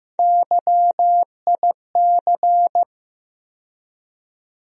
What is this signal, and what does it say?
Morse code, heard as a keyed tone, about 15 wpm, 700 Hz. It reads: YIC
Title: USA Call Sign Postfixes - 3 Letters 15wpm